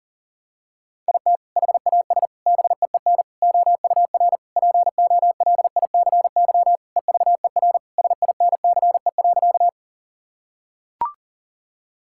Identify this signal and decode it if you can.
Morse code, heard as a keyed tone, about 40 wpm, 700 Hz. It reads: IT HAS BEEN OUR POLICY EVER SINCE.
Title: Sentences from Top 1000 Words - Review Part 5 - 40wpm